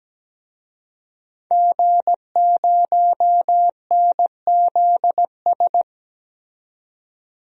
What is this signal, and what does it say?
Morse code, heard as a keyed tone, about 17 wpm, 700 Hz. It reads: G0NZS
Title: DXCC-100 Call Signs - 5 Characters - 17wpm